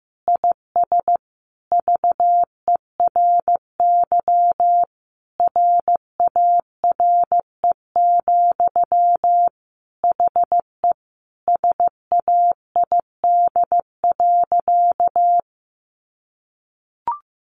Morse code, heard as a keyed tone, about 15 wpm, 700 Hz. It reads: IS VERY RARE, HE SAID.
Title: Sentences from Top 1100 Words 15wpm